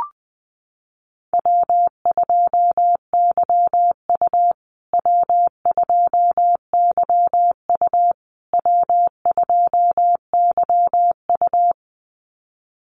Morse code, heard as a keyed tone, about 20 wpm, 700 Hz. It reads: W2YU W2YU W2YU
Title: USA Call Signs Repeated 3x 20wpm